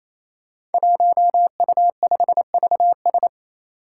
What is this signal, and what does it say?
Morse code, heard as a keyed tone, about 27 wpm, 700 Hz. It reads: 1U5VS
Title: Five Challenging Letter-Number Combinations 28wpm